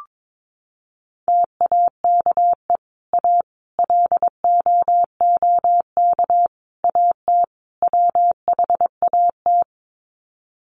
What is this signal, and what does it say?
Morse code, heard as a keyed tone, about 22 wpm, 700 Hz. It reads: TAKE A LOOK AT WHAT
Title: Sets of 5 Words Encore2 22wpm